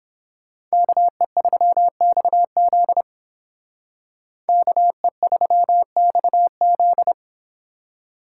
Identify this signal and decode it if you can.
Morse code, heard as a keyed tone, about 28 wpm, 700 Hz. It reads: KE3XZ KE3XZ